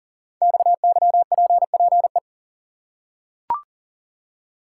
Morse code, heard as a keyed tone, about 38 wpm, 690 Hz. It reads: XYPPE
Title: Code Groups - 5 Letters - 40wpm